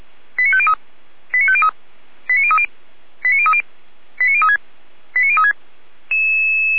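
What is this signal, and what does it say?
Five-tone selective call (ZVEI), heard as quick runs of five tones, tones of about 70 ms. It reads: 89503 89503 89030 89030 89036 89036
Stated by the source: ZVEI five-tone call received by SDR, recorded by fire brigade Feuerwehr Orlamünde.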